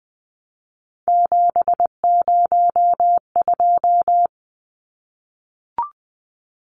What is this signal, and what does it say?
Morse code, heard as a keyed tone, about 20 wpm, 700 Hz. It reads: 702
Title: Numbers - 3-Digit 20wpm